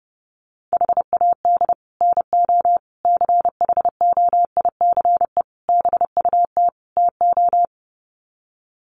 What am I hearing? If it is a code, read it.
Morse code, heard as a keyed tone, about 30 wpm, 700 Hz. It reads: HAD NO CHOICE BUT TO
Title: Sets of 5 Words 30wpm